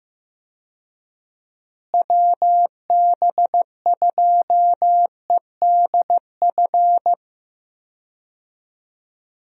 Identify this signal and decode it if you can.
Morse code, heard as a keyed tone, about 15 wpm, 700 Hz. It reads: WB2EDF
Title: USA Call Signs - 6 Characters - Repeated 2x - 15wpm